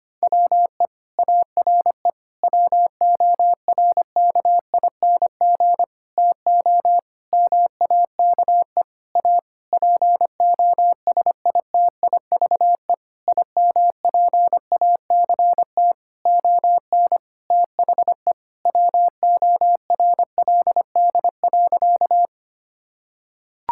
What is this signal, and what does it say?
Morse code, heard as a keyed tone, about 25 wpm, 700 Hz. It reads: WE ARE WORKING TO MAKE A POSITIVE IMPACT ON THE WORLD.
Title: Sentences from Top 1000 Words - Review Part 6 - 25wpm